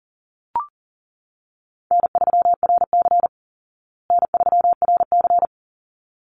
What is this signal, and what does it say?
Morse code, heard as a keyed tone, about 40 wpm, 700 Hz. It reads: N3RC N3RC
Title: Easy Call Signs Repeated 2x - 40wpm